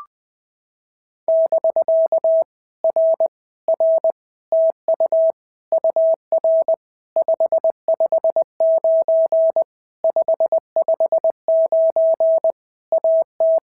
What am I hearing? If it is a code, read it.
Morse code, heard as a keyed tone, about 20 wpm, 650 Hz. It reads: <BK> R R TU UR 559 559 AT
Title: POTA QSO Practice - 20wpm